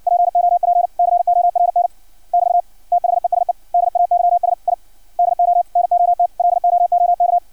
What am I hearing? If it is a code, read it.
Morse code, heard as a keyed tone, about 56 wpm, 700 Hz. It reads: POP CORN <BT> THESE DAYS I DO NOT LOOK
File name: archived / CRSnet